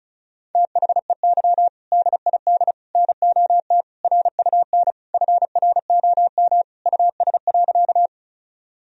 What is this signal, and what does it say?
Morse code, heard as a keyed tone, about 36 wpm, 700 Hz. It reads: THEY DID NOT RUN FROM US.
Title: Sentences from Top 1000 Words - Review Part 1 - 35wpm